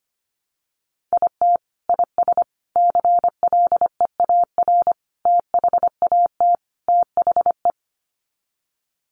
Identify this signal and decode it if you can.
Morse code, heard as a keyed tone, about 25 wpm, 700 Hz. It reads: IT IS CLEAR THAT THE
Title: Sets of 5 Words Encore 25wpm